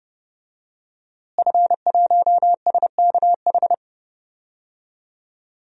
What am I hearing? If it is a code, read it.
Morse code, heard as a keyed tone, about 30 wpm, 700 Hz. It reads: F1SKH
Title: DXCC-100 Call Signs - 5 Characters - Repeated 3x - 30wpm